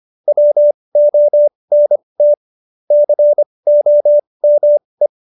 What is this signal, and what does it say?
Morse code, heard as a keyed tone, about 25 wpm, spaced as for 20 wpm, 590 Hz. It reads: WONT COME
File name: Jokes-25-char-20-eff